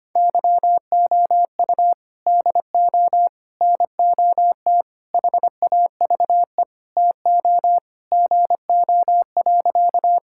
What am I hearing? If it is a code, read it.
Morse code, heard as a keyed tone, about 25 wpm, 700 Hz. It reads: YOU DO NOT HAVE TO GO.
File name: practice / 25wpm / Sentences with 3-Word Phrases - 25wpm